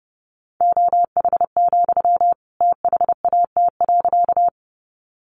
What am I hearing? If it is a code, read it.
Morse code, heard as a keyed tone, about 30 wpm, 700 Hz. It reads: OH, THAT.
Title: Sentences from Top 500 Words 30wpm